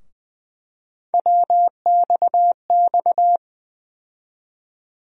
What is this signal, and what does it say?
Morse code, heard as a keyed tone, about 20 wpm, 700 Hz. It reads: WXX
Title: USA Call Sign Postfixes - 3 Letters 20wpm